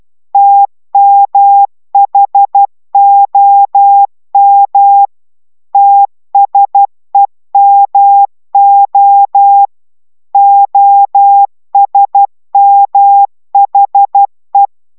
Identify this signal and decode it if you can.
Morse code, heard as a keyed tone, about 12 wpm, 800 Hz. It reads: TMHOM TSEMO OSMHE